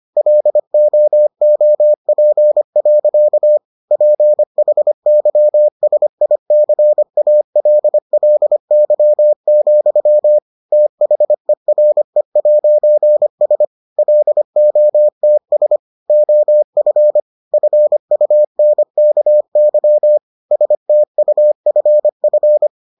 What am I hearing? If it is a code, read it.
Morse code, heard as a keyed tone, about 25 wpm, 600 Hz. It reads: LOOP. PHYSICALLY, THERE'S LOTS OF FUNKY STUFF